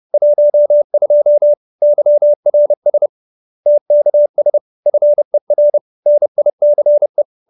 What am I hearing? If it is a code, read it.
Morse code, heard as a keyed tone, about 30 wpm, 590 Hz. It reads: 12 YRS TKS FER NICE
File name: QSO315 30